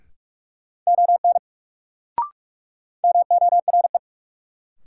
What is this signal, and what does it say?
Morse code, heard as a keyed tone, about 42 wpm, 700 Hz. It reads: ON MORE
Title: Sentences from Top 100 Words - New Word Drill - 45wpm